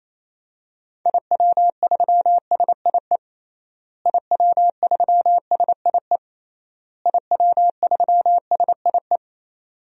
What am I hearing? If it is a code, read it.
Morse code, heard as a keyed tone, about 28 wpm, 700 Hz. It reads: IW3SIE IW3SIE IW3SIE
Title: DXCC-100 Call Signs - 6 Characters - Repeated 3x - 28wpm